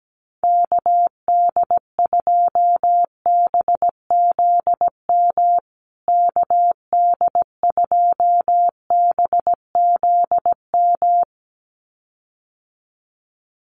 Morse code, heard as a keyed tone, about 17 wpm, 700 Hz. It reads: KD2BZM KD2BZM
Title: USA Call Signs Repeated 2x 17wpm